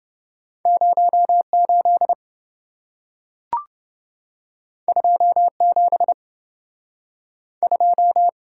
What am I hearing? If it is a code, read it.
Morse code, heard as a keyed tone, about 27 wpm, spaced as for 4 wpm, 700 Hz. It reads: 0 82 72